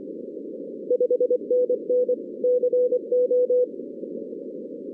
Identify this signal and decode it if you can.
Morse code, heard as a keyed tone, about 24 wpm, 500 Hz. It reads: 5NN CO
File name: 2016 arrl dx cw / 2016-ARRL-DX-CW-K5TR- - 0-39